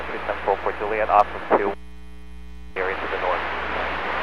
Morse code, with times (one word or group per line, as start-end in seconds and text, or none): none